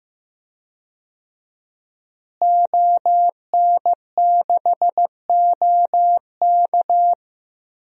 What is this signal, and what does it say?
Morse code, heard as a keyed tone, about 15 wpm, 700 Hz. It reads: ON6OK
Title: DXCC-100 Call Signs - 5 Characters - Repeated 2x - 15wpm